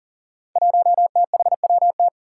Morse code, heard as a keyed tone, about 40 wpm, 700 Hz. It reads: 1THWT
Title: Five Challenging Letter-Number Combinations 40wpm